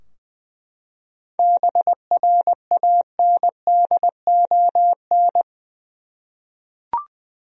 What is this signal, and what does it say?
Morse code, heard as a keyed tone, about 20 wpm, 700 Hz. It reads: BRANDON
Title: USA Common Names - 20wpm